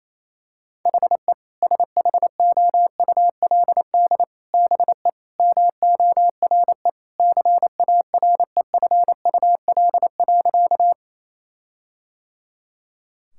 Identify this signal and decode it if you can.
Morse code, heard as a keyed tone, about 28 wpm, 700 Hz. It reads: HE SHOULD BE MORE CAREFUL.
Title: Sentences from Top 1600 Words 28wpm